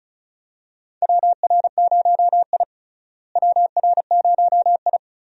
Morse code, heard as a keyed tone, about 34 wpm, 700 Hz. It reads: WR0I WR0I
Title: USA Call Signs - 4 Characters - Repeated 3x - 35wpm